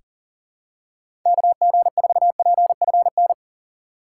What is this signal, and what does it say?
Morse code, heard as a keyed tone, about 39 wpm, 700 Hz. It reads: KG4PFN
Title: USA Call Signs - 2x Character Spacing - 40wpm